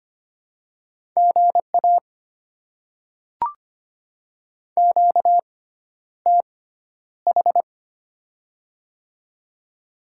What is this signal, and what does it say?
Morse code, heard as a keyed tone, about 25 wpm, spaced as for 11 wpm, 700 Hz. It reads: GA QTH
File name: Top 20 QSO Elements - 6x Character Spacing - 25wpm